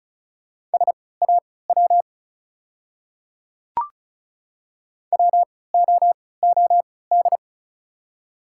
Morse code, heard as a keyed tone, about 36 wpm, spaced as for 20 wpm, 700 Hz. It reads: SAW WOOD
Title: 3x Character Spacing - Top 500 Words - 35wpm